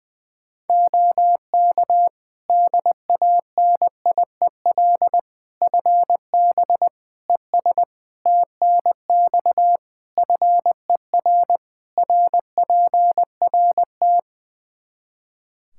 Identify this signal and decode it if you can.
Morse code, heard as a keyed tone, about 20 wpm, 700 Hz. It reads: OK DANIEL FB ES TNX FER RPRT
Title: K1USN SST QSO Practice 20wpm